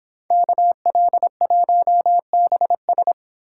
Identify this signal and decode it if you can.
Morse code, heard as a keyed tone, about 29 wpm, 700 Hz. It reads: KL1BS